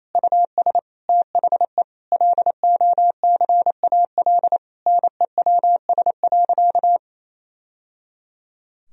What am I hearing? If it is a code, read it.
Morse code, heard as a keyed tone, about 28 wpm, 700 Hz. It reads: US THE LOCAL NEWS.
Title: Sentences from Top 1000 Words - Review Part 3 - 28wpm